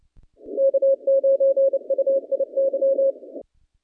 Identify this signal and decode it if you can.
Morse code, heard as a keyed tone, about 29 wpm, 540 Hz. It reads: K9UIY
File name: contests / naqp-cw-k5tr-op-k5pi- -0300